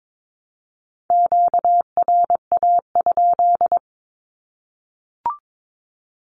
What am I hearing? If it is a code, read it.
Morse code, heard as a keyed tone, about 22 wpm, 700 Hz. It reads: QRA?
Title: Q-Codes Repeated 3x 22wpm